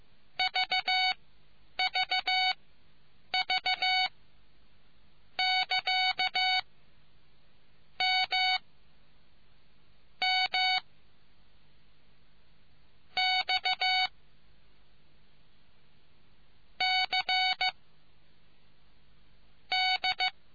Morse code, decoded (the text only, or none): VVV<KA> M M X C D